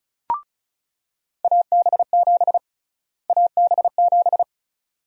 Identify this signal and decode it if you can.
Morse code, heard as a keyed tone, about 34 wpm, 700 Hz. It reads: AB7 AB7
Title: USA Call Sign Prefixes - Repeated 2x - 35wpm